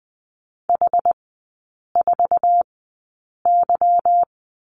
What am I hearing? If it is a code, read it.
Morse code, heard as a keyed tone, about 20 wpm, spaced as for 8 wpm, 700 Hz. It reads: H4Y
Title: 159 Lesson 26e - Letter g - Three Characters 20wpm